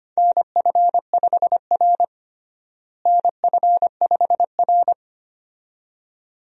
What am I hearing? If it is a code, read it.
Morse code, heard as a keyed tone, about 25 wpm, 700 Hz. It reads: NF5R NF5R
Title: USA Call Signs - 4 Characters - Repeated 2x - 25wpm